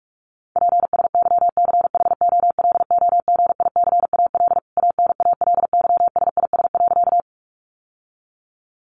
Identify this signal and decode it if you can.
Morse code, heard as a keyed tone, about 44 wpm, 700 Hz. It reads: PSYCHOLOGICAL ANALYSIS.
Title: Sentences from Top 1400 Words 45wpm